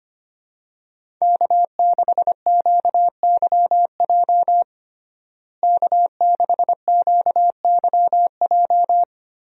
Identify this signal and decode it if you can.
Morse code, heard as a keyed tone, about 25 wpm, 700 Hz. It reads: K6QYJ K6QYJ
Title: USA Call Signs - 5 Characters - Repeated 3x - 25wpm